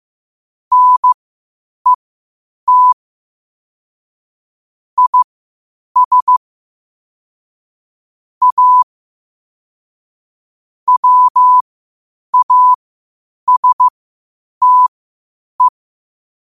Morse code, heard as a keyed tone, about 15 wpm, spaced as for 8 wpm, 1000 Hz. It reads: NET IS A WASTE
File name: qotdfiles / 8wpm